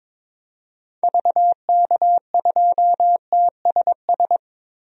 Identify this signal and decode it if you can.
Morse code, heard as a keyed tone, about 21 wpm, 700 Hz. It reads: VK2TSS
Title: DXCC-100 Call Signs - 6 Characters - Repeated 2x - 22wpm